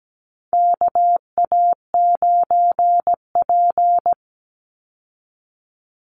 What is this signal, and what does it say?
Morse code, heard as a keyed tone, about 17 wpm, 700 Hz. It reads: KA9P
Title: Call Signs Repeated 2x - 17wpm